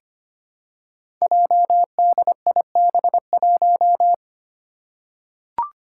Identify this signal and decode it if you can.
Morse code, heard as a keyed tone, about 25 wpm, 700 Hz. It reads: JDIB1
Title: Five Challenging Letter-Number Combinations 25wpm